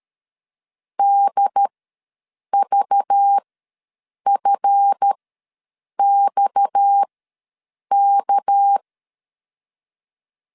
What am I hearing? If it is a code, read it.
Morse code, heard as a keyed tone, about 13 wpm, spaced as for 7 wpm, 800 Hz. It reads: DVFXK
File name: LEZIONI / 7 GF9Z / 7 GF ws 12wpm 3140